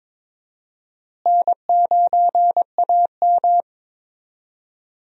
Morse code, heard as a keyed tone, about 21 wpm, 700 Hz. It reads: N9AM
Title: USA Call Signs - 4 Characters - Repeated 2x - 22wpm